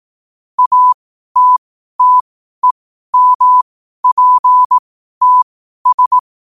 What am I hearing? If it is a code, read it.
Morse code, heard as a keyed tone, about 18 wpm, spaced as for 12 wpm, 1000 Hz. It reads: ATTEMPTS
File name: qotdfiles / 12wpm